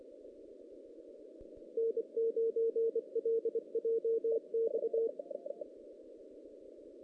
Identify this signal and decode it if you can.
Morse code, heard as a keyed tone, about 24 wpm, 460 Hz. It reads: N9LJX
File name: 2016 JAN CW NAQP